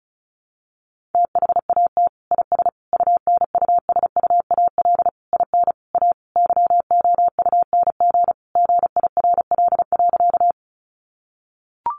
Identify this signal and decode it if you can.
Morse code, heard as a keyed tone, about 34 wpm, 700 Hz. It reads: THAT IS UNUSUAL IN A YOUNG GIRL.
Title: Sentences from Top 1400 Words 35wpm